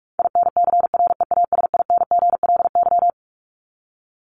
Morse code, heard as a keyed tone, about 44 wpm, 700 Hz. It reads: INCREASINGLY
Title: LY Suffix - Top 100 Words with Base Words 45wpm